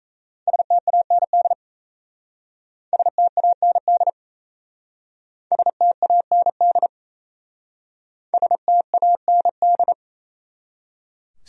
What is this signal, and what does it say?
Morse code, heard as a keyed tone, about 37 wpm, 700 Hz. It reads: STAND STAND STAND STAND